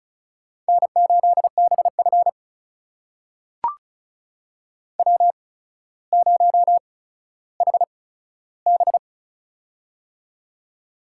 Morse code, heard as a keyed tone, about 36 wpm, spaced as for 15 wpm, 700 Hz. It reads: N8BF W0HB
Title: USA Call Signs - 4 Characters - 8x Character Spacing - 35wpm